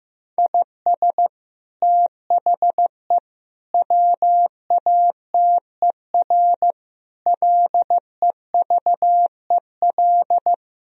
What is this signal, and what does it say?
Morse code, heard as a keyed tone, about 15 wpm, 700 Hz. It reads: IS THE WATER LEVEL
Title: Sentences from Top 1000 Words - Review Part 4 - 15wpm